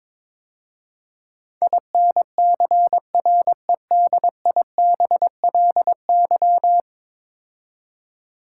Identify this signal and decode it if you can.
Morse code, heard as a keyed tone, about 22 wpm, 700 Hz. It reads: INCREDIBLY
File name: practice / 22wpm / 10 Letter Words - First 3 Letters Given - 22wpm